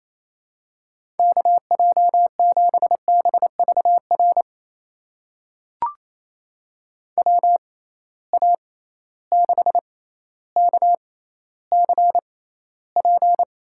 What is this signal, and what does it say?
Morse code, heard as a keyed tone, about 28 wpm, spaced as for 14 wpm, 700 Hz. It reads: KJ7BVR WA6KCP
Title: USA Call Signs - 6 Characters - 6x Character Spacing - 28wpm